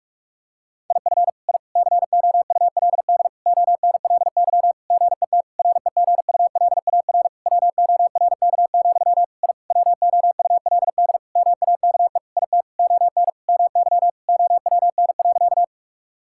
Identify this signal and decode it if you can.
Morse code, heard as a keyed tone, about 41 wpm, 700 Hz. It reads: IF I COULD ONLY GET REGULAR WORK, I WOULD MAKE IT ON MY OWN.
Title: Sentences from Top 1000 Words - Review Part 6 - 45wpm